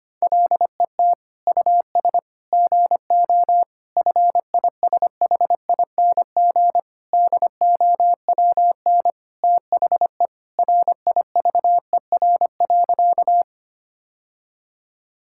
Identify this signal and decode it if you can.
Morse code, heard as a keyed tone, about 25 wpm, 700 Hz. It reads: LET US GO FISHING DOWN THE RIVER.
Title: Sentences from Top 1000 Words - Review Part 5 - 25wpm